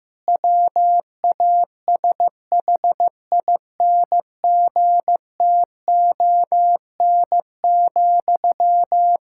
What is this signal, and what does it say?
Morse code, heard as a keyed tone, about 15 wpm, 700 Hz. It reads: WASHINGTON,
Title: Practice QSO - Exchange 1 - Beginning - 15wpm